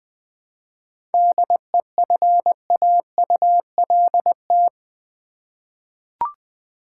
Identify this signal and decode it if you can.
Morse code, heard as a keyed tone, about 20 wpm, 700 Hz. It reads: DEFAULT
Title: 7 Letter Words - First 3 Letters Given - 20wpm